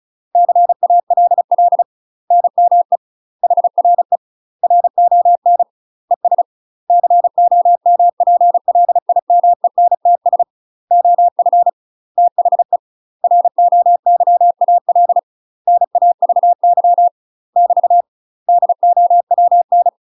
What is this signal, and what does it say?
Morse code, heard as a keyed tone, about 35 wpm, 700 Hz. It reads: CALL NME HRE RON ES COMPLIMENTS OF THE ROYAL NAVY <BT> DOWN
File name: UK MASTER QSOs TRAINING / 35WPM / UK MASTER QSOs 0017